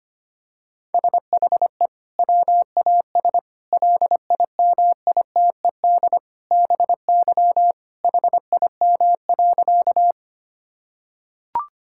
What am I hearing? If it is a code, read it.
Morse code, heard as a keyed tone, about 25 wpm, 700 Hz. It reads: SHE WAS LIMITED BY HIM.
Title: Sentences from Top 1000 Words - Review Part 6 - 25wpm